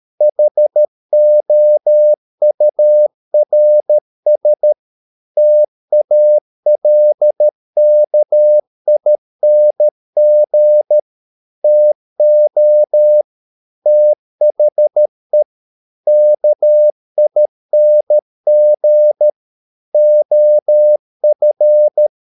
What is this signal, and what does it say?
Morse code, heard as a keyed tone, about 13 wpm, 600 Hz. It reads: HOURS TALKING TO THE KING OF